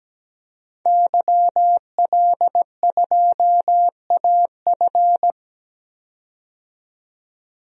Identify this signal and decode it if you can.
Morse code, heard as a keyed tone, about 17 wpm, 700 Hz. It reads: YL2AF
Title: DXCC-100 Call Signs - 5 Characters - 17wpm